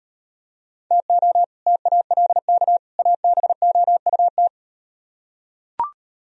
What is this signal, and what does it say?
Morse code, heard as a keyed tone, about 38 wpm, 700 Hz. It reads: TO TALK ABOUT